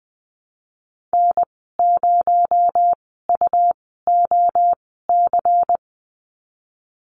Morse code, heard as a keyed tone, about 20 wpm, spaced as for 15 wpm, 700 Hz. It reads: N0UOC
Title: 219 Lesson 35g - Number 0 - Call Signs 20wpm